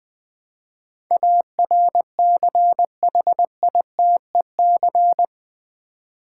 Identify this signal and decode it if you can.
Morse code, heard as a keyed tone, about 20 wpm, 700 Hz. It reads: ARCHITEC
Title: Build A Word - 4 Letters At A Time - 12-Letter Words - 20wpm